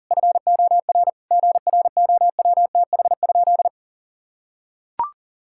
Morse code, heard as a keyed tone, about 40 wpm, 700 Hz. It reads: FOR GROWTH?
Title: Sentences from Top 1000 Words - Review Part 5 - 40wpm